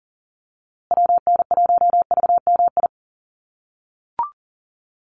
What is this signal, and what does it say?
Morse code, heard as a keyed tone, about 40 wpm, 700 Hz. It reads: WN1VMI
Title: USA Call Signs Repeated 3x 40wpm